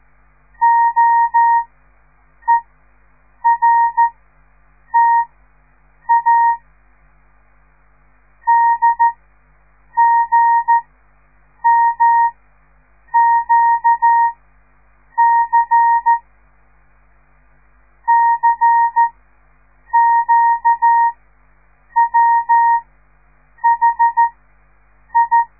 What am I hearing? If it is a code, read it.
Morse code, heard as a keyed tone, about 13 wpm, spaced as for 7 wpm, 930 Hz. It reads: OERTA DGMQC CQWHI
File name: radiomorningstar / morse / les 6